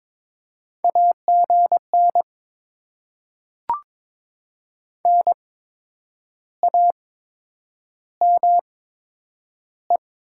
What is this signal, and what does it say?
Morse code, heard as a keyed tone, about 22 wpm, spaced as for 8 wpm, 700 Hz. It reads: AGN NAME
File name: Top 20 QSO Elements - 8x Character Spacing - 22wpm